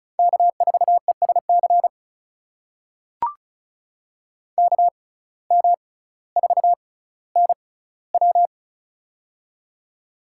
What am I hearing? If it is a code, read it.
Morse code, heard as a keyed tone, about 34 wpm, spaced as for 18 wpm, 700 Hz. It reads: K4ESC KM4NW
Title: USA Call Signs - 5 Characters - 6x Character Spacing - 35wpm